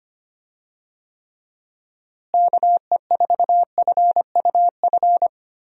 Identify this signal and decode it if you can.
Morse code, heard as a keyed tone, about 25 wpm, 700 Hz. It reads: KE4FUF
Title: USA Call Signs - 6 Characters - Repeated 2x - 25wpm